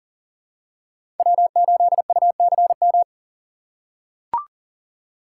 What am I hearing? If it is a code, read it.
Morse code, heard as a keyed tone, about 40 wpm, 700 Hz. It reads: W8UCM
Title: USA Call Signs - 5 Characters - 6x Character Spacing - 40wpm